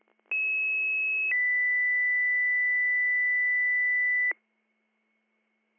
Two-tone page, heard as a one-second tone, then a three-second tone, about 2600 then 2000 Hz. It is heard from 0.3 to 4.3 s.